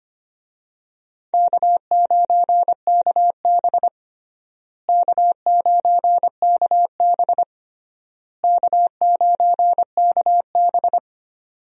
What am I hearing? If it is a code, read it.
Morse code, heard as a keyed tone, about 25 wpm, 700 Hz. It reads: K9KB K9KB K9KB
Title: USA Call Signs - 4 Characters - Repeated 3x - 25wpm